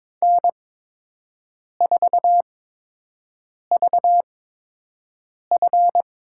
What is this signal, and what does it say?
Morse code, heard as a keyed tone, about 22 wpm, spaced as for 6 wpm, 700 Hz. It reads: N4VF
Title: USA Call Signs - 5 Characters - 8x Character Spacing - 22wpm